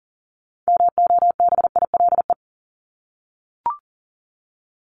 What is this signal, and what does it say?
Morse code, heard as a keyed tone, about 40 wpm, 700 Hz. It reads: MOBILE
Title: Top 1751-2000 Words 40wpm